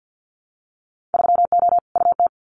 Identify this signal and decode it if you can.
Morse code, heard as a keyed tone, about 50 wpm, 700 Hz. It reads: 3O UT
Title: Winter Field Day Exchanges 50wpm